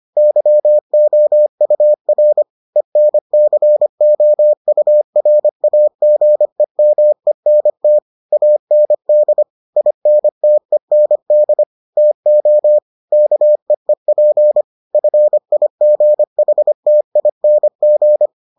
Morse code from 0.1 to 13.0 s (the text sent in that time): YOUR ENCOURAGEMENT AND INTEND TO